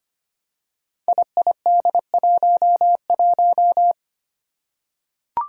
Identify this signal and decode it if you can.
Morse code, heard as a keyed tone, about 25 wpm, 700 Hz. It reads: IID11
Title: Five Challenging Letter-Number Combinations 25wpm